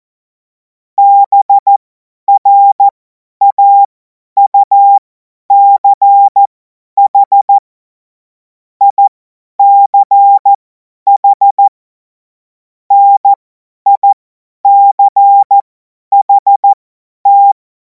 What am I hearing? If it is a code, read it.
Morse code, heard as a keyed tone, about 14 wpm, spaced as for 10 wpm, 800 Hz. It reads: BRAUCH ICH NICHT
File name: Morsekurs / Witze / Witz 31. März 2026 50 F70